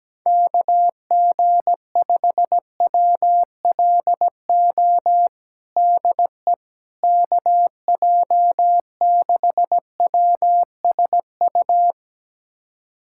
Morse code, heard as a keyed tone, about 17 wpm, 700 Hz. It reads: KG5WLO DE KJ6WSU
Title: USA Callsign DE Callsign 17wpm